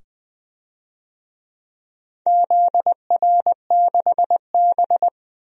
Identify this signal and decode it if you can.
Morse code, heard as a keyed tone, about 20 wpm, 700 Hz. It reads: ZR6B
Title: DXCC-100 Call Signs - 4 Characters - 20wpm